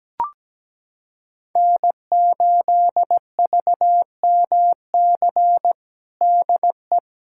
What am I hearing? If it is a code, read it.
Morse code, heard as a keyed tone, about 17 wpm, 700 Hz. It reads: N8VMC DE
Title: USA Callsign DE Callsign - 5 Characters - 17wpm